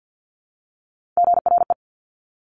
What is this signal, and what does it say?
Morse code, heard as a keyed tone, about 50 wpm, 700 Hz. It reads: GRE
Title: Top Three Letter Combinations in Words 50wpm